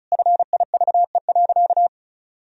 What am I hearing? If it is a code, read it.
Morse code, heard as a keyed tone, about 36 wpm, 700 Hz. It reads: FIVE.
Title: Sentences from Top 400 Words and Abbreviations 35wpm